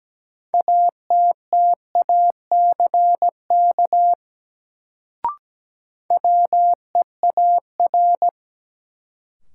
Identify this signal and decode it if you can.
Morse code, heard as a keyed tone, about 17 wpm, 700 Hz. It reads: ATTACK WEAR
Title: Sentences from Top 1500 Words - New Word Drill Alt - 17wpm